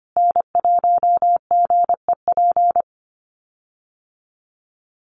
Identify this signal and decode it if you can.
Morse code, heard as a keyed tone, about 25 wpm, 700 Hz. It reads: N1GEP
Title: USA Call Signs Repeated 3x 25wpm